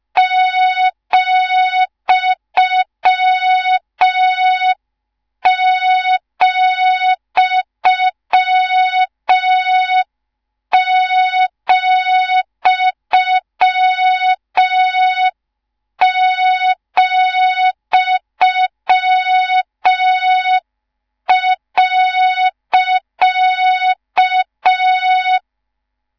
Morse code, heard as a keyed tone, about 5 wpm, 3700 Hz, with noise. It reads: ,,,,.